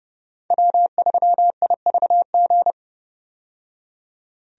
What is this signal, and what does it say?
Morse code, heard as a keyed tone, about 31 wpm, 700 Hz. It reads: W3IVG
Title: USA Call Signs - 5 Characters - 30wpm